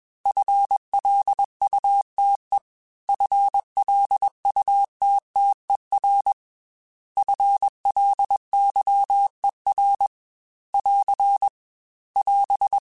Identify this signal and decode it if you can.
Morse code, heard as a keyed tone, about 21 wpm, 800 Hz. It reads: FLUTE FLUTTER FLYER <AR> <AS>
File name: FL Runs / FL Runs mp3s / FL Runs @20